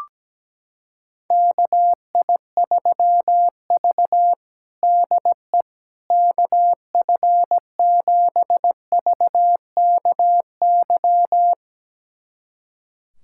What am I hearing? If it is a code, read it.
Morse code, heard as a keyed tone, about 17 wpm, 700 Hz. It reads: KI3V DE KF7VKY
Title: USA Callsign DE Callsign 17wpm